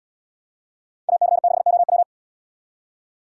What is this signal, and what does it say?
Morse code, heard as a keyed tone, about 71 wpm, 700 Hz. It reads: AC6OW